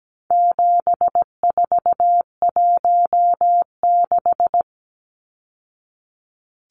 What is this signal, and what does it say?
Morse code, heard as a keyed tone, about 17 wpm, 700 Hz. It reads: 7416
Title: Numbers - 4-Digit 17wpm